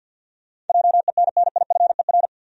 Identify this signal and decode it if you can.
Morse code, heard as a keyed tone, about 48 wpm, 700 Hz. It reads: JENNIFER
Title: USA Common Names - 50wpm